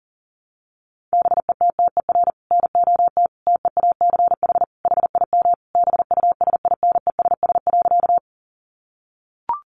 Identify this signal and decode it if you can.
Morse code, heard as a keyed tone, about 40 wpm, 700 Hz. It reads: BETTER NOT TEACH HIM BUSINESS.
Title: Sentences from Top 600 Words 40wpm